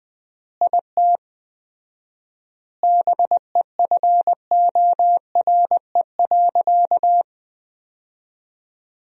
Morse code, heard as a keyed tone, about 20 wpm, 700 Hz. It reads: IT BEFORE.
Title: 4x Word-Spacing - Sentences from Top 400 Words - 20wpm